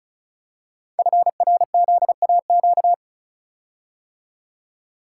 Morse code, heard as a keyed tone, about 35 wpm, 700 Hz. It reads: FRZAQ
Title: Code Groups - 5 Letters - 35wpm